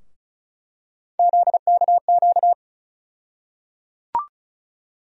Morse code, heard as a keyed tone, about 36 wpm, 700 Hz. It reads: ZKQ
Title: USA Call Sign Postfixes - 3 Letters 35wpm